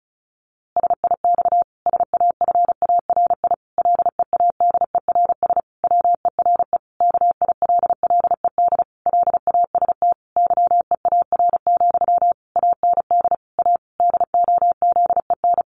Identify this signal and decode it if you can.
Morse code, heard as a keyed tone, about 34 wpm, 700 Hz. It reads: SIX SAFARI LEADERS WERE KILLED LAST YEAR, AND A DOZEN
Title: A Sound Of Thunder 35wpm